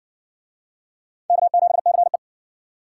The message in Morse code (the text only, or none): K7CE